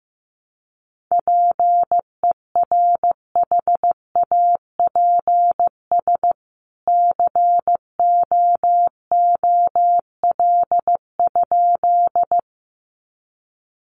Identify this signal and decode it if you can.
Morse code, heard as a keyed tone, about 15 wpm, 700 Hz. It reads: PERHAPS COOL?
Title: Sentences from Top 1200 Words 15wpm